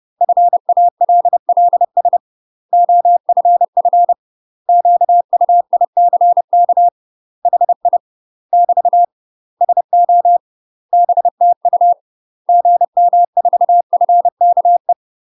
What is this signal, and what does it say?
Morse code, heard as a keyed tone, about 30 wpm, 700 Hz. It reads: FALLS OFF QUICK HI <BT> SO BTU GM4FKE
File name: UK MASTER QSOs TRAINING / 30WPM / UK MASTER QSOs 0019